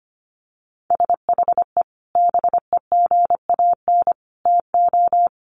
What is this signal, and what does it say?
Morse code, heard as a keyed tone, about 25 wpm, 700 Hz. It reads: SHE BEGAN TO
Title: Sentences from Top 1200 Words 25wpm